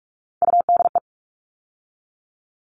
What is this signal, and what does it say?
Morse code, heard as a keyed tone, about 44 wpm, 700 Hz. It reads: UDE
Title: All Cut Numbers - 3-Digit 45wpm